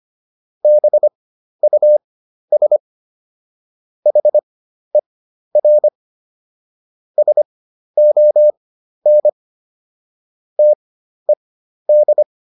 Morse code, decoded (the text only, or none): BUS HER SON TED